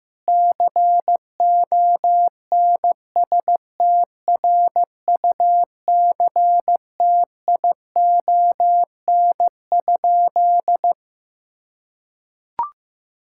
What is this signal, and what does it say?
Morse code, heard as a keyed tone, about 15 wpm, 700 Hz. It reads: CONSTRUCTION?
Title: Sentences from Top 1600 Words 15wpm